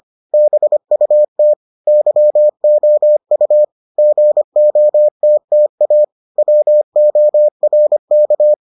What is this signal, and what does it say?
Morse code, heard as a keyed tone, about 25 wpm, 600 Hz. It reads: BUT YOU GOTTA WORK